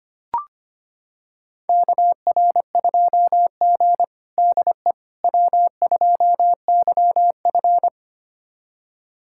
Morse code, heard as a keyed tone, about 25 wpm, 700 Hz. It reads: KR2G DE W2YF
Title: USA Callsign DE Callsign - 4 Characters - 25wpm